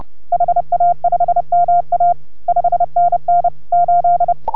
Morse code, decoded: HA5MA 5NN 8

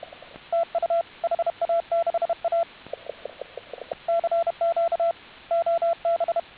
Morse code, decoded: TU HA6A CQ OB

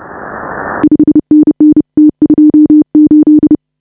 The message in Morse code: 5NN T28

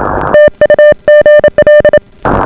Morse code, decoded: TU GL

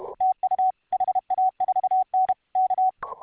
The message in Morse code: TU HA4N K

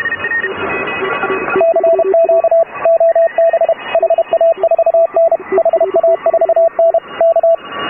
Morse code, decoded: CQ OB HA4N HA4N K

